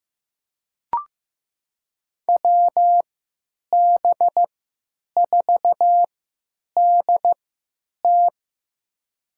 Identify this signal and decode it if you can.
Morse code, heard as a keyed tone, about 15 wpm, spaced as for 9 wpm, 700 Hz. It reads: WB4DT
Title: USA Call Signs - 3x Character Spacing - 15wpm